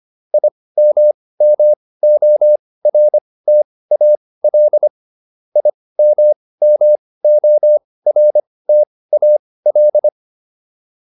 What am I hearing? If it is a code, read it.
Morse code, heard as a keyed tone, about 25 wpm, spaced as for 18 wpm, 600 Hz. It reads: IMMORTAL IMMORTAL